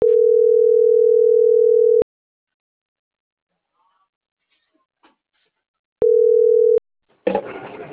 Ringback tone (North America), 440 + 480 Hz.
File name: CENTRAL MAYAB / out esmeralda.garcia